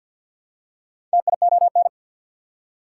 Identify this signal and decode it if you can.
Morse code, heard as a keyed tone, about 52 wpm, 690 Hz. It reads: TION
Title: Prefixes and Suffixes 50wpm